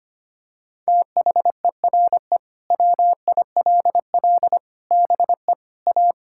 Morse code, heard as a keyed tone, about 25 wpm, 700 Hz. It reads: THERE WILL BE A
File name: practice / 25wpm / Sentences with 3-Word Phrases - 25wpm